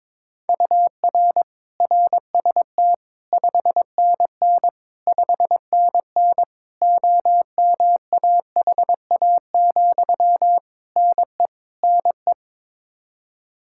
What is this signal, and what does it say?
Morse code, heard as a keyed tone, about 22 wpm, 700 Hz. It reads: UR RST 5NN 5NN OMAHA, NE NE
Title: Practice QSO - Exchange 1 - Beginning - 22wpm